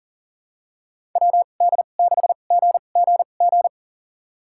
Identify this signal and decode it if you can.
Morse code, heard as a keyed tone, about 40 wpm, spaced as for 29 wpm, 700 Hz. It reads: WD6GGG
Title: USA Call Signs - 6 Characters - 2x Character Spacing - 40wpm